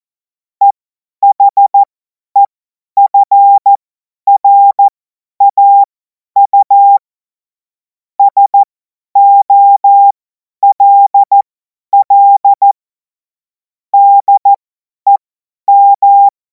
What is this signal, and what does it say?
Morse code, heard as a keyed tone, about 14 wpm, spaced as for 10 wpm, 800 Hz. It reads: EHEFRAU SOLL DEM
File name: Morsekurs / Witze / Witz 24. Februar 2026 50 F70